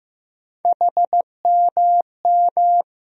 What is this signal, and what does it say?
Morse code, heard as a keyed tone, about 15 wpm, 700 Hz. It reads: HMM
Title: USA Call Sign Postfixes - 3 Letters 15wpm